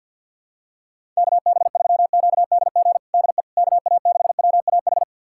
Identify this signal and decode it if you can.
Morse code, heard as a keyed tone, about 52 wpm, 700 Hz. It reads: KB3QDG DE KA6WAH
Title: USA Callsign DE Callsign - 6 Characters - 50wpm